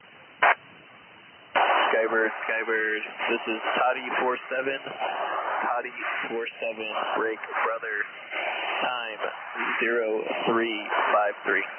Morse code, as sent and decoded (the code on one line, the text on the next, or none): none